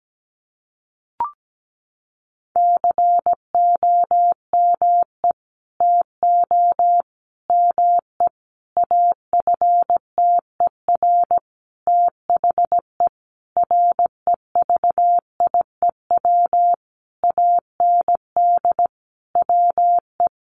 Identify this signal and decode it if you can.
Morse code, heard as a keyed tone, about 17 wpm, 700 Hz. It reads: COME TO ME AFTER THE REVIEW AND WE
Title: Sentences from Top 900 Words 17wpm